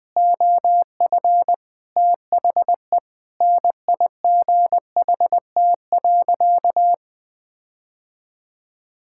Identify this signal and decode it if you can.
Morse code, heard as a keyed tone, about 20 wpm, 700 Hz. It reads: OF THE NIGHT.
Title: Sentences with 8-Word Phrases - 20wpm